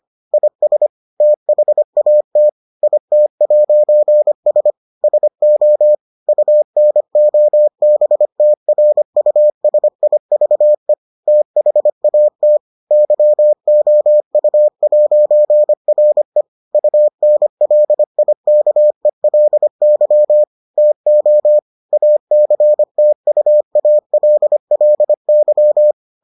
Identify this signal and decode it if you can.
Morse code, heard as a keyed tone, about 25 wpm, 600 Hz. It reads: IS THAT IT'S SO UNOBTRUSIVE THAT YOU'RE UNLIKELY TO ACTUALLY